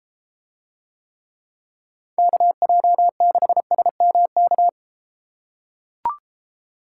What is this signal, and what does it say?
Morse code, heard as a keyed tone, about 30 wpm, 700 Hz. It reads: KJ6SMK